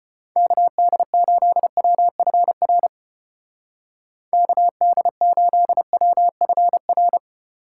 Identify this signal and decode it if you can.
Morse code, heard as a keyed tone, about 30 wpm, 700 Hz. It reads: KD8WFR KD8WFR
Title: Speed Racing - USA 2x3 Call Signs 25wpm